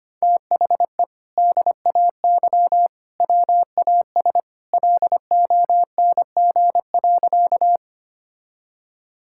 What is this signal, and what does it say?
Morse code, heard as a keyed tone, about 25 wpm, 700 Hz. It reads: THE DAY WAS LONG.
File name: practice / 25wpm / Sentences with 2-Word Phrases - 25wpm